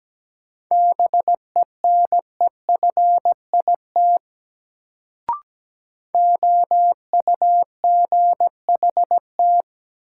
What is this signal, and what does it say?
Morse code, heard as a keyed tone, about 17 wpm, 700 Hz. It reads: BENEFIT OUGHT
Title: Sets of 5 Words Encore - New Word Drill - 17wpm